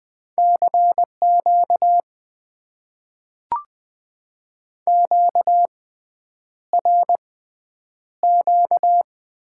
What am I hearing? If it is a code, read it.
Morse code, heard as a keyed tone, about 20 wpm, spaced as for 9 wpm, 700 Hz. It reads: CQ QRQ
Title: Top 90 QSO Elements - 6x Character Spacing - 20wpm